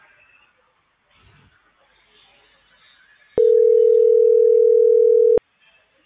Ringback tone (North America), 440 + 480 Hz.